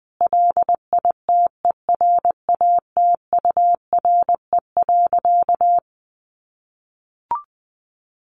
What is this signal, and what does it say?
Morse code, heard as a keyed tone, about 20 wpm, 700 Hz. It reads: LITERATURE.
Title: Sentences from Top 900 Words 20wpm